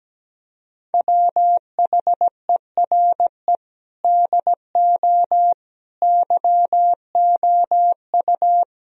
Morse code, heard as a keyed tone, about 17 wpm, 700 Hz. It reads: WHERE DO YOU
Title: Sentences from Top 1000 Words - Review Part 2 - 17wpm